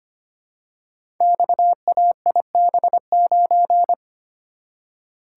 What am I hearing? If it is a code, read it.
Morse code, heard as a keyed tone, about 25 wpm, 700 Hz. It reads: XAIB9
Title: Code Groups - 5 Letter:Numbers - 25wpm